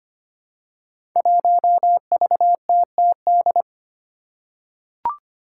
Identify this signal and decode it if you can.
Morse code, heard as a keyed tone, about 25 wpm, 700 Hz. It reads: 1VTTD
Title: Five Challenging Letter-Number Combinations 25wpm